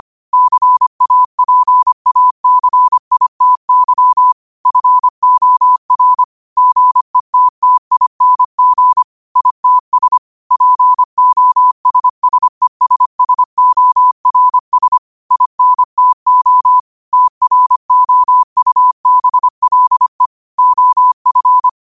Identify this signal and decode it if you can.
Morse code, heard as a keyed tone, about 25 wpm, 1000 Hz. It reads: CAPACITY FOR GETTING ITS POSSESSORS INTO TROUBLE OF